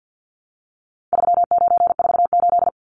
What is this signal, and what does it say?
Morse code, heard as a keyed tone, about 50 wpm, 700 Hz. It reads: 3948
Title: Numbers - 4-Digit 50wpm